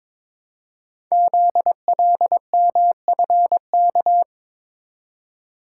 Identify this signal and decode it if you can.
Morse code, heard as a keyed tone, about 22 wpm, 700 Hz. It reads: ZLMFK
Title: Code Groups - 5 Letters - 22wpm